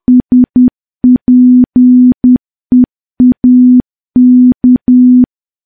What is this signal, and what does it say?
Morse code, heard as a keyed tone, about 10 wpm, 250 Hz. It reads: SPEAK
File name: morse / morse-training-10WPM